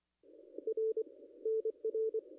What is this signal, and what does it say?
Morse code, heard as a keyed tone, about 24 wpm, 440 Hz. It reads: R NR
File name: contests / ss-cw-k5tr-op-n5rz -0200